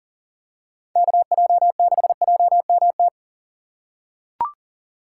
Morse code, heard as a keyed tone, about 40 wpm, 700 Hz. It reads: KJ6JMT
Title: USA Call Signs - 6 Characters - 6x Character Spacing - 40wpm